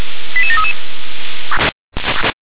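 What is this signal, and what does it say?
Five-tone selective call (ZVEI), heard as quick runs of five tones, tones of about 70 ms. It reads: 88733
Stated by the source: ZVEI five-tone call 88500 received by SDR.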